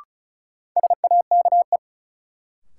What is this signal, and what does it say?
Morse code, heard as a keyed tone, about 36 wpm, 700 Hz. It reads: SAKE
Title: Sets of 4 Words - New Word Drill - 35wpm